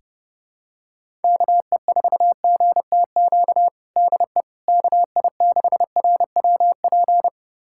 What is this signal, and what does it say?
Morse code, heard as a keyed tone, about 32 wpm, 700 Hz. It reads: KE4GTQ DE KI6RWP
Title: USA Callsign DE Callsign - 6 Characters - 30wpm